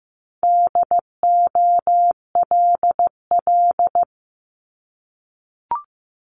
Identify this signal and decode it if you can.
Morse code, heard as a keyed tone, about 15 wpm, 700 Hz. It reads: DOLL
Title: Multi-letter Phonemes Repeated 3x 15wpm